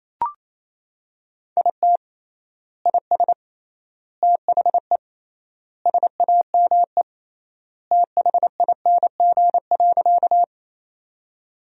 Morse code, heard as a keyed tone, about 28 wpm, 700 Hz. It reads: IT IS THE SAME THING.
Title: 3x Word-Spacing - Sentences from Top 200 Words - 28wpm